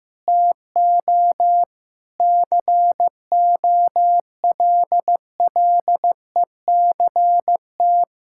Morse code, heard as a keyed tone, about 15 wpm, 700 Hz. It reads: TO COLLECT
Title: Sentences from Top 1600 Words 15wpm